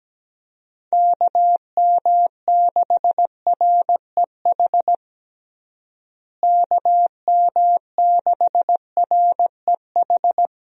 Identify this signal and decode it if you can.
Morse code, heard as a keyed tone, about 17 wpm, 700 Hz. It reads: KM6REH KM6REH
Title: USA Call Signs - 6 Characters - Repeated 2x - 17wpm